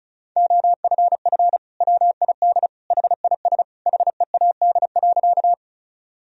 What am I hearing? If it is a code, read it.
Morse code, heard as a keyed tone, about 36 wpm, 700 Hz. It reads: OFF WID HIS HEAD.
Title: Sentences from Top 300 Words and Abbreviations 35wpm